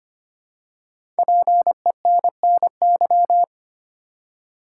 Morse code, heard as a keyed tone, about 25 wpm, 700 Hz. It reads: PENNY
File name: 5 Letter Words - First 3 Letters Given - 25wpm